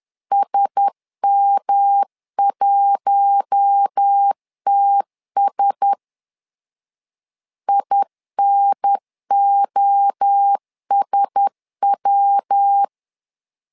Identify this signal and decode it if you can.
Morse code, heard as a keyed tone, about 10 wpm, 800 Hz. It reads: SM1TS INOSW
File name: LEZIONI / 4 Q1S 10wpm 2133